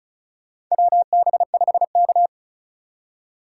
Morse code, heard as a keyed tone, about 34 wpm, 700 Hz. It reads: WB5K
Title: K1USN SST QSO Practice - 35wpm